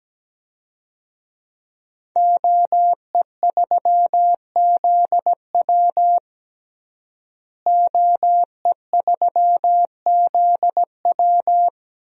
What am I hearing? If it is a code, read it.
Morse code, heard as a keyed tone, about 17 wpm, 700 Hz. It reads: OE3ZW OE3ZW
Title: DXCC-100 Call Signs - 5 Characters - Repeated 3x - 17wpm